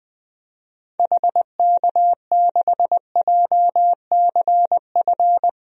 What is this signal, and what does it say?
Morse code, heard as a keyed tone, about 21 wpm, 700 Hz. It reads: HK6JCF
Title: DXCC-100 Call Signs - 6 Characters - Repeated 2x - 20wpm